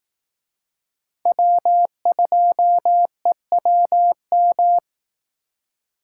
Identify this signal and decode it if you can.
Morse code, heard as a keyed tone, about 18 wpm, 700 Hz. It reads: W2EWM
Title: Speed Racing - USA 1x3 Call Signs 15wpm